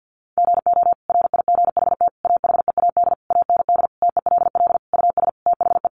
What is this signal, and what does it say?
Morse code, heard as a keyed tone, about 50 wpm, 700 Hz. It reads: GO RIGHT AHEAD AND TELL US THE
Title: Sentences from Top 1100 Words 50wpm